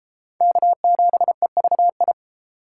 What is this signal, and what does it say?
Morse code, heard as a keyed tone, about 29 wpm, 700 Hz. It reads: K7EVI